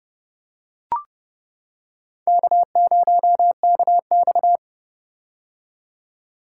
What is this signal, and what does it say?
Morse code, heard as a keyed tone, about 30 wpm, 700 Hz. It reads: K0KX
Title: K1USN SST QSO Practice - 30wpm